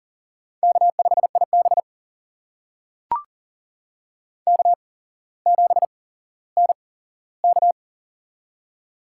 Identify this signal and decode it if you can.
Morse code, heard as a keyed tone, about 40 wpm, spaced as for 17 wpm, 700 Hz. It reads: K5IB K7NK
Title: USA Call Signs - 4 Characters - 8x Character Spacing - 40wpm